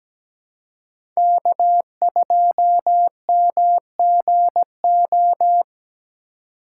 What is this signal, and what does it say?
Morse code, heard as a keyed tone, about 17 wpm, 700 Hz. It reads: K2MGO
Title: USA Call Signs - 5 Characters - 4x Character Spacing - 17wpm